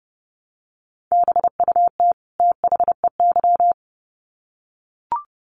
Morse code, heard as a keyed tone, about 30 wpm, 700 Hz. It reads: BUT THEY
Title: Sets of 2 Words 30wpm